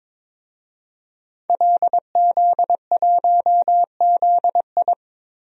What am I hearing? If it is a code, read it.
Morse code, heard as a keyed tone, about 22 wpm, 700 Hz. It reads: LZ1ZI
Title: DXCC-100 Call Signs - 5 Characters - 22wpm